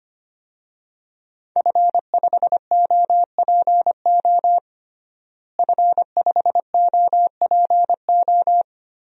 Morse code, heard as a keyed tone, about 25 wpm, 700 Hz. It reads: F5OPO F5OPO
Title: DXCC-100 Call Signs - 5 Characters - Repeated 3x - 25wpm